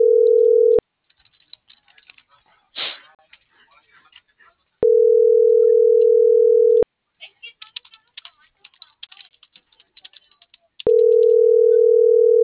Ringback tone (North America), 440 + 480 Hz.